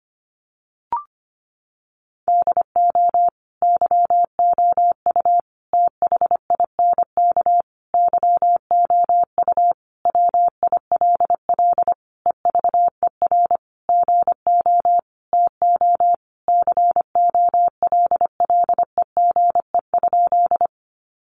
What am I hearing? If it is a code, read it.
Morse code, heard as a keyed tone, about 25 wpm, 700 Hz. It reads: DO YOU THINK YOU WILL EVER GO TO COLLEGE?
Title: Sentences from Top 1300 Words 25wpm